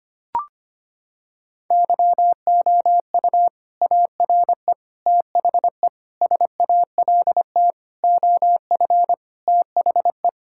Morse code, heard as a keyed tone, about 25 wpm, 700 Hz. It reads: YOU ARE THE SALT OF THE
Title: Sentences from Top 1000 Words - Review Part 5 - 25wpm